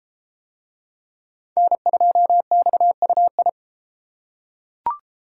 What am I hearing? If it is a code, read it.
Morse code, heard as a keyed tone, about 31 wpm, 700 Hz. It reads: N2XUI